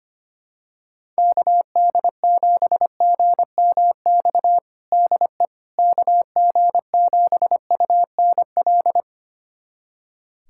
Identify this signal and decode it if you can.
Morse code, heard as a keyed tone, about 25 wpm, 700 Hz. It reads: KD7GMX DE KG7UNL
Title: USA Callsign DE Callsign - 6 Characters - 25wpm